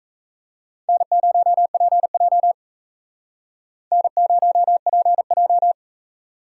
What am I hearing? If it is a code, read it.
Morse code, heard as a keyed tone, about 33 wpm, 700 Hz. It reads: N0PJ N0PJ
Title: Speed Racing - USA 1x2 Call Signs 28wpm